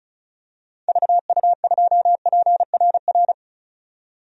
Morse code, heard as a keyed tone, about 34 wpm, 700 Hz. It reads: VU2PRR
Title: DXCC-100 Call Signs - 6 Characters - 35wpm